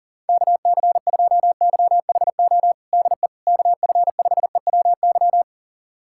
Morse code, heard as a keyed tone, about 40 wpm, 700 Hz. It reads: KC2YHO DE KF5EWY
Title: USA Callsign DE Callsign - 6 Characters - 40wpm